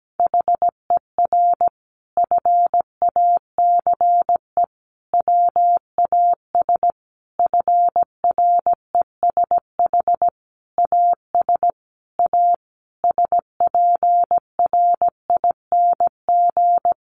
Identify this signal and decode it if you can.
Morse code, heard as a keyed tone, about 17 wpm, 700 Hz. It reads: HER FACE WAS FRESH AS A SPRING
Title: Sentences from Top 1200 Words 17wpm